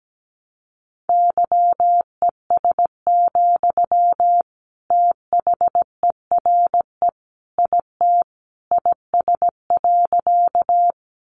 Morse code, heard as a keyed tone, about 17 wpm, 700 Hz. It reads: YES, THERE IT IS.
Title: Sentences from Top 500 Words 17wpm